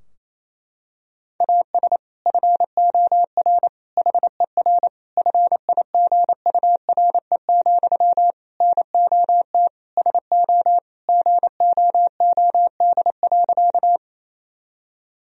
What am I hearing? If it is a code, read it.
Morse code, heard as a keyed tone, about 28 wpm, 700 Hz. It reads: AS FOR HER FIGURE, NOT SO GOOD.
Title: Sentences from Top 1000 Words - Review Part 2 - 28wpm